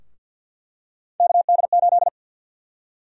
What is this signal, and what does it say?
Morse code, heard as a keyed tone, about 49 wpm, 700 Hz. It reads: KD8
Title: USA Call Sign Prefixes - Repeated 2x - 50wpm